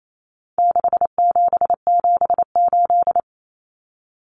Morse code, heard as a keyed tone, about 28 wpm, 700 Hz. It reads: 6778
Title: Numbers - 4-Digit 28wpm